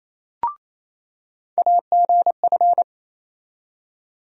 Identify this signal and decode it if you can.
Morse code, heard as a keyed tone, about 28 wpm, 700 Hz. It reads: AGF
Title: USA Call Sign Postfixes - 3 Letters 28wpm